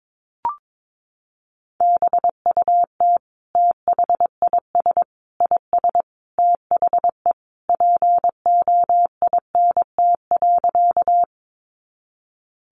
Morse code, heard as a keyed tone, about 22 wpm, 700 Hz. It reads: BUT THIS IS THE POINT.
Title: Sentences from Top 200 Words 22wpm